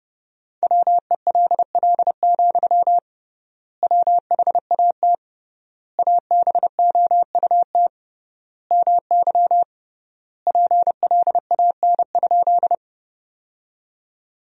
Morse code, heard as a keyed tone, about 30 wpm, 700 Hz. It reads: WELL, WHAT ABOUT MY PLAN?
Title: 3x Word-Spacing - Sentences from Top 500 Words - 30wpm